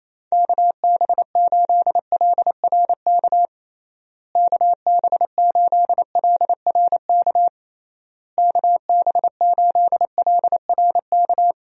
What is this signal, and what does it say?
Morse code, heard as a keyed tone, about 28 wpm, 700 Hz. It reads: KB8LRK KB8LRK KB8LRK
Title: USA Call Signs - 6 Characters - Repeated 3x - 28wpm